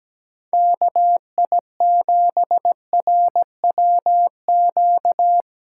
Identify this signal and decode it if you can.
Morse code, heard as a keyed tone, about 17 wpm, 700 Hz. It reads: KI7RWQ
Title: USA Callsign DE Callsign 17wpm